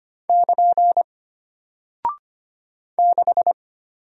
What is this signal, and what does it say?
Morse code, heard as a keyed tone, about 25 wpm, 700 Hz. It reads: <KN> 6
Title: Single Character (Letter:Number:Punctuation:Prosign) Alt 25wpm